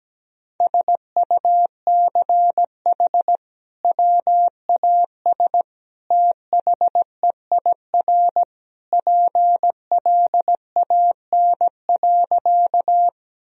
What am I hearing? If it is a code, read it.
Morse code, heard as a keyed tone, about 17 wpm, 700 Hz. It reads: SUCH WAS THEIR PLAN.
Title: Sentences from Top 1000 Words - Review Part 2 - 17wpm